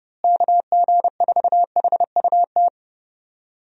Morse code, heard as a keyed tone, about 30 wpm, 700 Hz. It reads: KG4HUT
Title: USA Call Signs - 6 Characters - Repeated 3x - 30wpm